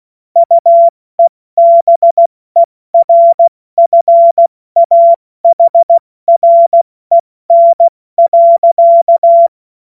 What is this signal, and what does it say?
Morse code, heard as a keyed tone, about 16 wpm, spaced as for 14 wpm, 680 Hz. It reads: UEBERFAHREN.
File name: Morsekurs / Witze / Witz 20. September 2022 70 F80